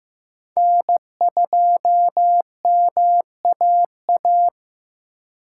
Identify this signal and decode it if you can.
Morse code, heard as a keyed tone, about 15 wpm, 700 Hz. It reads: N2MAA
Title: USA Call Signs - 4x Character Spacing - 15wpm